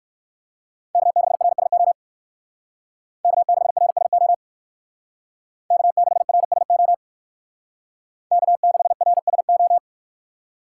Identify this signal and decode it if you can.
Morse code, heard as a keyed tone, about 58 wpm, 700 Hz. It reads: K6RSO K6RSO K6RSO K6RSO